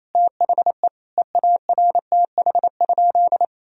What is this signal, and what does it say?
Morse code, heard as a keyed tone, about 28 wpm, 700 Hz. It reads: THE EARTH?
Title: Sentences from Top 1000 Words - Review Part 3 - 28wpm